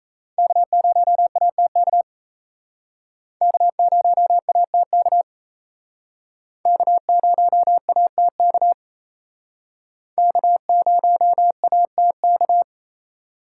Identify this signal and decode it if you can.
Morse code, heard as a keyed tone, about 36 wpm, 700 Hz. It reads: K0ATK K0ATK K0ATK K0ATK